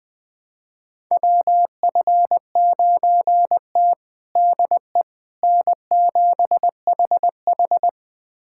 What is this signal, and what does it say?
Morse code, heard as a keyed tone, about 21 wpm, 700 Hz. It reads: WF9T DE N7HH
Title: USA Callsign DE Callsign - 4 Characters - 20wpm